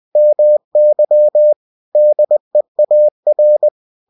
Morse code, heard as a keyed tone, about 20 wpm, 600 Hz. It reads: MY DEAR
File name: cw-books / winnie-the-pooh / winnie-20wpm-page-0082